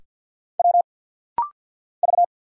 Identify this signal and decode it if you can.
Morse code, heard as a keyed tone, about 50 wpm, 700 Hz. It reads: W V
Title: Single Letters - Rapid-Fire 50wpm